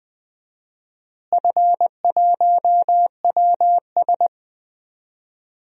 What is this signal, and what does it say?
Morse code, heard as a keyed tone, about 21 wpm, 690 Hz. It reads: F1WS
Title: DXCC-100 Call Signs - 4 Characters - Repeated 3x - 20wpm